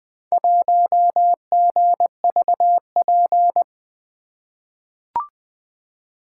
Morse code, heard as a keyed tone, about 20 wpm, 700 Hz. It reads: 1GVP
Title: Four Letter-Number Combinations - 20wpm